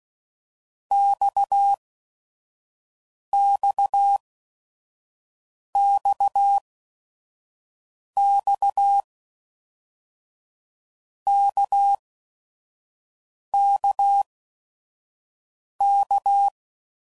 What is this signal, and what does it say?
Morse code, heard as a keyed tone, about 16 wpm, spaced as for 5 wpm, 800 Hz. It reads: XXXX KKK
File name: Basic Letter Groups By7Groups / RKPX / RKPX@5 f15